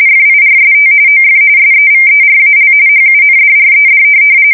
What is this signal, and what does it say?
RTTY, about 45.45 baud. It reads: FIRST OF SEVERAL HUNDRED
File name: fsk170 / USS-Missouri